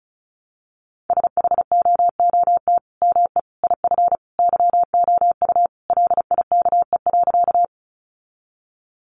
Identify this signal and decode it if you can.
Morse code, heard as a keyed tone, about 34 wpm, 700 Hz. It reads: SHOOT ME IF YOU LIKE.
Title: Sentences from Top 1200 Words 35wpm